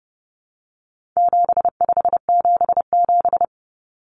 Morse code, heard as a keyed tone, about 29 wpm, 700 Hz. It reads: 7577
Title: Common Cut Numbers - 4-Digit 30wpm